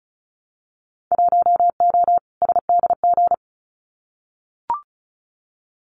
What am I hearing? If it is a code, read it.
Morse code, heard as a keyed tone, about 36 wpm, 700 Hz. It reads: 1O SDG
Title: Winter Field Day Exchanges Encore 35wpm